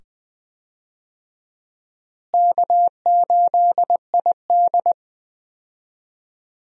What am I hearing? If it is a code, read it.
Morse code, heard as a keyed tone, about 20 wpm, 700 Hz. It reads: K8ID
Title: USA Call Signs - 4 Characters - Repeated 2x - 20wpm